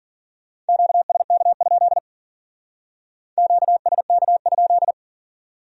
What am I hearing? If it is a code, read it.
Morse code, heard as a keyed tone, about 44 wpm, 700 Hz. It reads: QSK? QSK?
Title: Speed-Racing Q-Codes - 40wpm